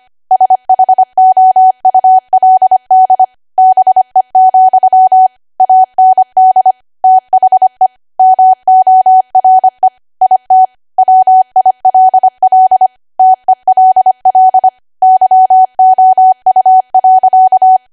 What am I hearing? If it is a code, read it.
Morse code, heard as a keyed tone, about 25 wpm, 750 Hz. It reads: SHOULD BE, AND THE MORE IT WILL TELL YOU.
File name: Archive / 25 WPM / 25WPM